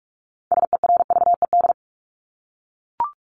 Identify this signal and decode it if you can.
Morse code, heard as a keyed tone, about 44 wpm, 700 Hz. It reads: SERVED
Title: ED Suffix - Top 100 Words with Base Words 45wpm